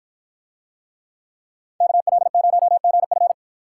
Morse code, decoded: KL0GF

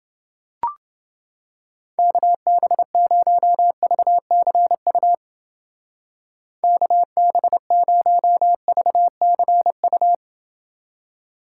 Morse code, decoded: KB0VCU KB0VCU

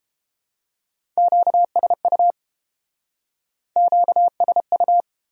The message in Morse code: QSU QSU

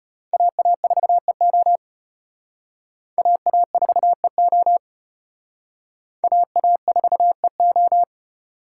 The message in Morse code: AA4EO AA4EO AA4EO

